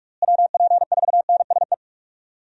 Morse code, heard as a keyed tone, about 41 wpm, 700 Hz. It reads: WP4NSE
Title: USA Call Signs - 6 Characters - Repeated 2x - 45wpm